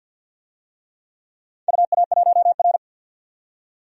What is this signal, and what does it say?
Morse code, heard as a keyed tone, about 49 wpm, 700 Hz. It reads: UA1R
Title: DXCC-100 Call Signs - 4 Characters - 50wpm